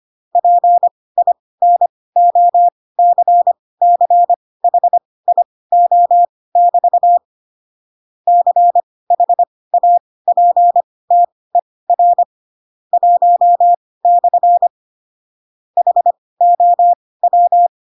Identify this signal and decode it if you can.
Morse code, heard as a keyed tone, about 25 wpm, spaced as for 18 wpm, 700 Hz. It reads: PINOCCHIO<BT> CHAPTER 1/ HOW
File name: Pinocchio / Pinocchio ch1 12 25 ik1hge